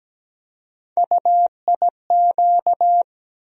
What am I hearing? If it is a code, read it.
Morse code, heard as a keyed tone, about 17 wpm, 700 Hz. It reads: UIQ
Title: USA Call Sign Postfixes - 3 Letters 17wpm